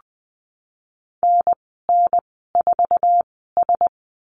Morse code, heard as a keyed tone, about 20 wpm, spaced as for 15 wpm, 700 Hz. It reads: NN4S
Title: 088 Lesson 15g - Number 5 - Call Signs 20wpm